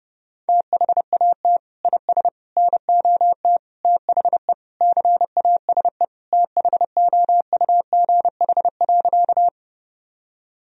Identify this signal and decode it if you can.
Morse code, heard as a keyed tone, about 32 wpm, 700 Hz. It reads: THAT IS NOT THE CASE THOUGH.
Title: Sentences from Top 1000 Words - Review Part 2 - 30wpm